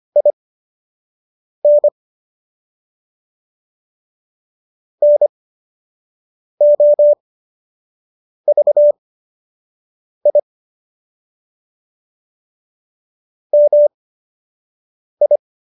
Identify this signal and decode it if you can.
Morse code, heard as a keyed tone, about 25 wpm, spaced as for 6 wpm, 590 Hz. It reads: IN NOVI MI